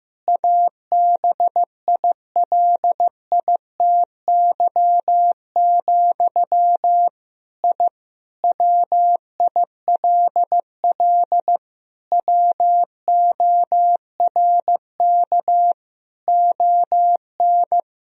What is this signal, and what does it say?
Morse code, heard as a keyed tone, about 15 wpm, 700 Hz. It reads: ABILITY, I WILL WORK ON
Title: Sentences from Top 1000 Words - Review Part 3 - 15wpm